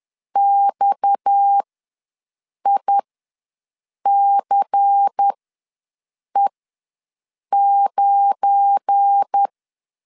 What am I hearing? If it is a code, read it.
Morse code, heard as a keyed tone, about 10 wpm, spaced as for 6 wpm, 800 Hz. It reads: XICE9
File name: LEZIONI / 7 GF9Z / 7 GF9 ws 10wpm 4034